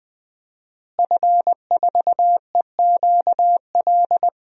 Morse code, heard as a keyed tone, about 20 wpm, 700 Hz. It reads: F4EQL
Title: DXCC-100 Call Signs - 5 Characters - Repeated 3x - 20wpm